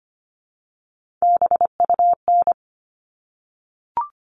Morse code, heard as a keyed tone, about 25 wpm, 700 Hz. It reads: BUN
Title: All Cut Numbers - 3-Digit 25wpm